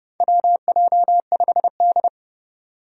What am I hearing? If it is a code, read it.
Morse code, heard as a keyed tone, about 30 wpm, 700 Hz. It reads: WJ5D
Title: USA Call Signs - 4 Characters - 30wpm